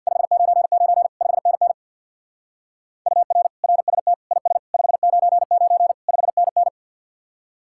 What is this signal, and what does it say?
Morse code, heard as a keyed tone, about 60 wpm, 700 Hz. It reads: 599 5NN UR RST IS 599 5NN
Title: Speed-Racing Common QSO Phrases - 50wpm